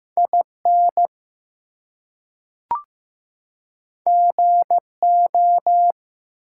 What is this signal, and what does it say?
Morse code, heard as a keyed tone, about 15 wpm, 700 Hz. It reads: IN GO
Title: Two Letter Words - 15wpm